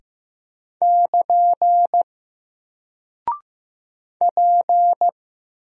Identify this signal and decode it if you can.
Morse code, heard as a keyed tone, about 15 wpm, 700 Hz. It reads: <KN> P
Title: Single Character (Letter:Number:Punctuation:Prosign) Alt 15wpm